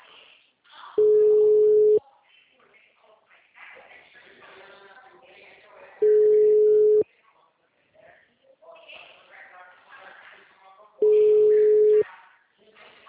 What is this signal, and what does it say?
Ringback tone (Europe, 425 Hz), with speech.